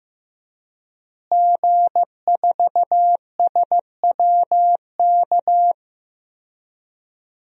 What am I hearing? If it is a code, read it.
Morse code, heard as a keyed tone, about 15 wpm, 700 Hz. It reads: G4SWK
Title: DXCC-100 Call Signs - 5 Characters - Repeated 3x - 15wpm